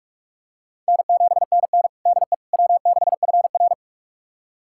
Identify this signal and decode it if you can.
Morse code, heard as a keyed tone, about 46 wpm, 700 Hz. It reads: N7NN DE W6FR
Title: USA Callsign DE Callsign - 4 Characters - 45wpm